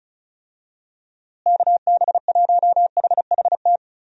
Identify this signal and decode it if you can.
Morse code, heard as a keyed tone, about 36 wpm, 700 Hz. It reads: KB1HHT
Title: USA Call Signs - 6 Characters - Repeated 3x - 35wpm